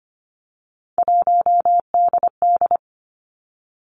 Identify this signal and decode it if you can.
Morse code, heard as a keyed tone, about 25 wpm, 700 Hz. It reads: 1DD
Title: Three Challenging Letter-Number Combinations 25wpm